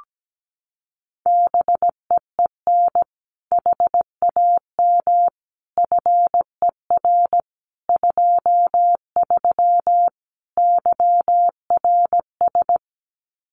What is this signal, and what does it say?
Morse code, heard as a keyed tone, about 17 wpm, 700 Hz. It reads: BEEN HAM FER 23 YRS
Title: Common QSO Phrases Encore 17wpm